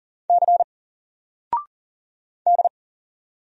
Morse code, heard as a keyed tone, about 40 wpm, 700 Hz. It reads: C D
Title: Single Letter-Number 40wpm